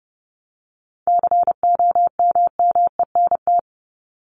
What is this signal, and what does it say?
Morse code, heard as a keyed tone, about 30 wpm, 700 Hz. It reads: COMMENT
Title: Top 1751-2000 Words 30wpm